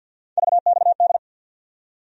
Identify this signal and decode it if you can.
Morse code, heard as a keyed tone, about 52 wpm, 700 Hz. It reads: VXD
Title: USA Call Sign Postfixes - 3 Letters 50wpm